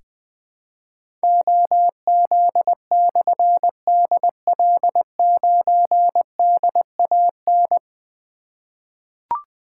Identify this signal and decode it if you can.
Morse code, heard as a keyed tone, about 20 wpm, 700 Hz. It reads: OZ/DL9DAN
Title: Difficult Call Signs 20wpm